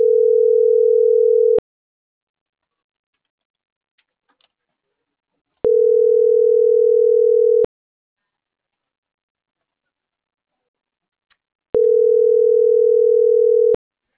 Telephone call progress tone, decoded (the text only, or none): ringback tone